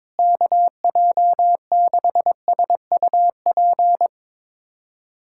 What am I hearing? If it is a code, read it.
Morse code, heard as a keyed tone, about 22 wpm, 700 Hz. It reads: KJ6SUP
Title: USA Callsign DE Callsign 22wpm